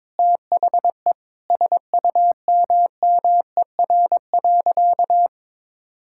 Morse code, heard as a keyed tone, about 22 wpm, 700 Hz. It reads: THE SUMMER.
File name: Sentences with 9-Word Phrases - 22wpm